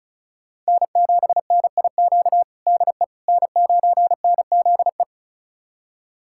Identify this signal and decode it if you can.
Morse code, heard as a keyed tone, about 34 wpm, 700 Hz. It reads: N7NIQ DE N9NZE
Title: USA Callsign DE Callsign - 5 Characters - 35wpm